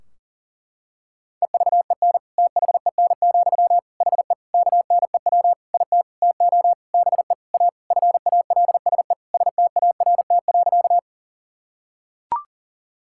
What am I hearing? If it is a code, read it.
Morse code, heard as a keyed tone, about 37 wpm, 700 Hz. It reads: EVEN THEN, HE KNEW IT TO BE A FALSE START.
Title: Sentences from Top 1600 Words 40wpm